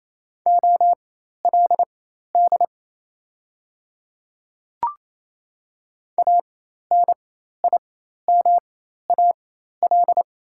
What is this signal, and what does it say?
Morse code, heard as a keyed tone, about 27 wpm, spaced as for 13 wpm, 690 Hz. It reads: OLD ANIMAL
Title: 4x Character Spacing - Top 200 Words - 28wpm